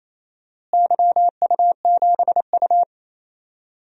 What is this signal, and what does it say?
Morse code, heard as a keyed tone, about 28 wpm, 700 Hz. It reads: YU7U
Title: DXCC-100 Call Signs - 4 Characters - Repeated 3x - 28wpm